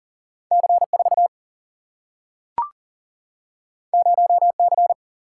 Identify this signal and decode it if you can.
Morse code, heard as a keyed tone, about 39 wpm, 700 Hz. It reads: C 4 0 C
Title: Single-Letter Single-Number 40wpm